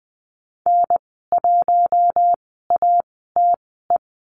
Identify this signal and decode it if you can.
Morse code, heard as a keyed tone, about 20 wpm, spaced as for 15 wpm, 700 Hz. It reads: N1ATE
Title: 062 Lesson 11g - Letter h - Call Signs 20wpm